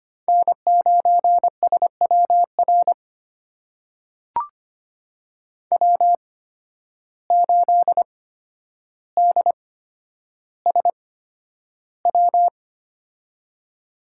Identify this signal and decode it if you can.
Morse code, heard as a keyed tone, about 25 wpm, spaced as for 11 wpm, 700 Hz. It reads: N9SWR W8DSW
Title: USA Call Signs - 5 Characters - 8x Character Spacing - 25wpm